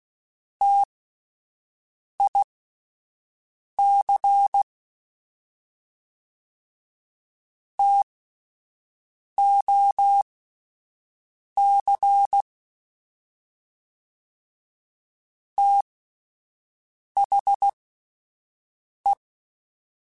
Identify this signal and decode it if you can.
Morse code, heard as a keyed tone, about 16 wpm, spaced as for 5 wpm, 800 Hz. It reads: TIC TOC THE